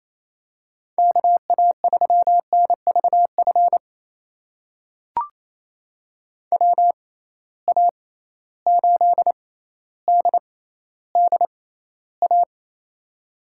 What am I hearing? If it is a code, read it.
Morse code, heard as a keyed tone, about 28 wpm, spaced as for 14 wpm, 700 Hz. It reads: KA3NVF WA8DDA
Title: USA Call Signs - 6x Character Spacing - 28wpm